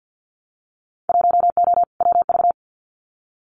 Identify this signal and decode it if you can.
Morse code, heard as a keyed tone, about 50 wpm, 700 Hz. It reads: 1O WV
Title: Winter Field Day Exchanges Encore 50wpm